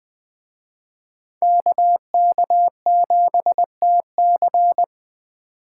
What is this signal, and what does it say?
Morse code, heard as a keyed tone, about 21 wpm, 700 Hz. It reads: KK7TC
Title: Speed Racing - USA 2x2 Call Signs 17wpm